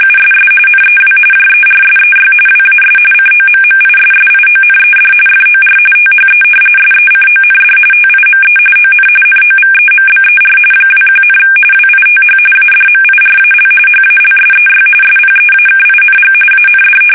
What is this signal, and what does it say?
RTTY, about 45.45 baud. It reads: USS NEW JERSEY WAS COMMISSIONED FOR THE SECOND TIME ON NOVEMBER 21, WITH ADMIRAL HALSEY SPEAKING AND
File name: fsk850 / battleshipnj